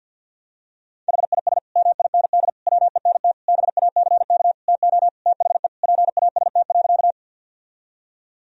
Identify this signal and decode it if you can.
Morse code, heard as a keyed tone, about 46 wpm, 700 Hz. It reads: HIS MIND WENT BACK TO THE PAST.
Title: Sentences from Top 1000 Words - Review Part 3 - 50wpm